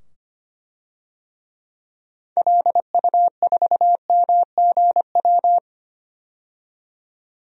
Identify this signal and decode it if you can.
Morse code, heard as a keyed tone, about 25 wpm, 700 Hz. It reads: LU4MGW
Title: DXCC-100 Call Signs - 6 Characters - 25wpm